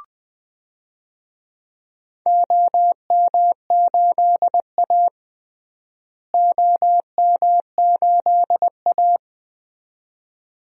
Morse code, heard as a keyed tone, about 20 wpm, 700 Hz. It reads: OM8A OM8A
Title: DXCC-100 Call Signs - 4 Characters - Repeated 2x - 20wpm